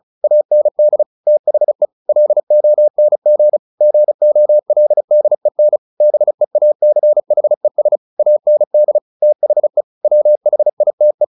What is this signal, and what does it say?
Morse code, heard as a keyed tone, about 34 wpm, 600 Hz. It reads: AND THE LONG GOLDEN BEACHES AND THE WHITE
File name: the old man and the sea / the old man and the sea 1 35wpm